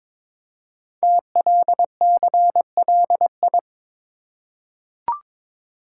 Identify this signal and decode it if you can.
Morse code, heard as a keyed tone, about 22 wpm, 700 Hz. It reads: TLCLI
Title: Code Groups - 5 Letter:Numbers - 22wpm